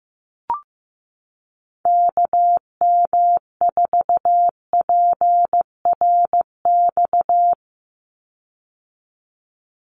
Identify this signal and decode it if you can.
Morse code, heard as a keyed tone, about 15 wpm, 700 Hz. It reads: KM4PRX
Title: USA Call Signs 15wpm